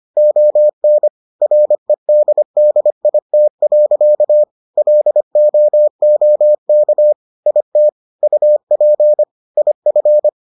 Morse code, heard as a keyed tone, about 25 wpm, 600 Hz. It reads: ON REDDIT. LOOK IT UP IF